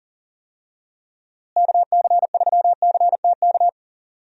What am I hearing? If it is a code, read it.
Morse code, heard as a keyed tone, about 40 wpm, 700 Hz. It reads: KC3CTK
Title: USA Call Signs - 6 Characters - Repeated 3x - 40wpm